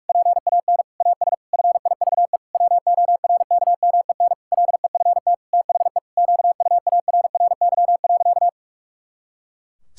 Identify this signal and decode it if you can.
Morse code, heard as a keyed tone, about 46 wpm, 700 Hz. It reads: PAN AS FIVE WORKMEN LEFT THE QUARRY.
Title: Pangram Challenge v2 - 45wpm